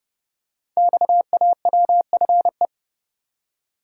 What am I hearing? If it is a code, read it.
Morse code, heard as a keyed tone, about 30 wpm, 700 Hz. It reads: XAWFE
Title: Code Groups - 5 Letters - 30wpm